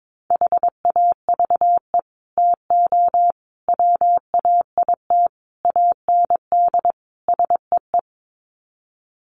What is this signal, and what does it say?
Morse code, heard as a keyed tone, about 22 wpm, 700 Hz. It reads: HAVE TO WAIT AND SEE
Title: Sets of 5 Words Encore2 22wpm